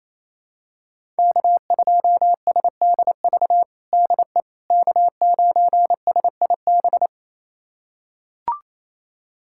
Morse code, heard as a keyed tone, about 28 wpm, 700 Hz. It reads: K2SDV DE K9SIB
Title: USA Callsign DE Callsign - 5 Characters - 28wpm